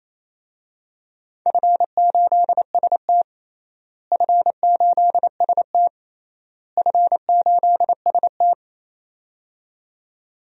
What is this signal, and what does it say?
Morse code, heard as a keyed tone, about 28 wpm, 700 Hz. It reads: F8ST F8ST F8ST
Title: DXCC-100 Call Signs - 4 Characters - Repeated 3x - 28wpm